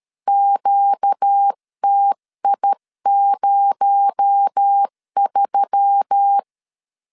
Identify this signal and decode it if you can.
Morse code, heard as a keyed tone, about 13 wpm, 800 Hz. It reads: QTI03
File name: CW / DET / 1 6 1500C S C 12wpm 3533